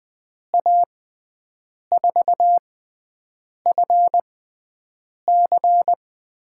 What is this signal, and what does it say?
Morse code, heard as a keyed tone, about 20 wpm, spaced as for 7 wpm, 700 Hz. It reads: A4FC
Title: USA Call Signs - 6x Character Spacing - 20wpm